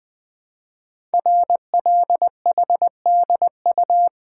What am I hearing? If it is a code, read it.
Morse code, heard as a keyed tone, about 20 wpm, 700 Hz. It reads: RLHDU
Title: Code Groups - 5 Letters - 20wpm